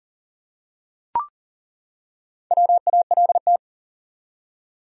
Morse code, heard as a keyed tone, about 38 wpm, 700 Hz. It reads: WALT
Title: USA Common Names - Max 4 Letters - 40wpm